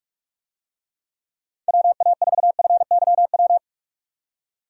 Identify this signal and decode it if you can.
Morse code, heard as a keyed tone, about 47 wpm, 700 Hz. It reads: WA4FYW
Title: USA Call Signs - 6 Characters - 45wpm